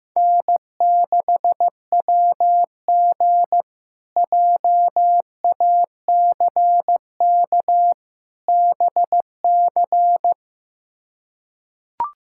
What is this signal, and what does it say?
Morse code, heard as a keyed tone, about 15 wpm, 700 Hz. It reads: N6WG JACK BC
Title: K1USN SST QSO Practice - 15wpm